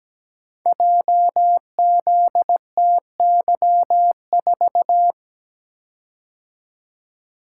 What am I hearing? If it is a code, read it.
Morse code, heard as a keyed tone, about 17 wpm, 700 Hz. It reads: JZTY4
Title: Code Groups - 5 Letter:Numbers - 17wpm